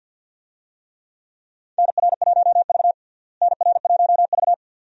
Morse code, heard as a keyed tone, about 49 wpm, 700 Hz. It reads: NR1V NR1V
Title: USA Call Signs - 4 Characters - Repeated 2x - 50wpm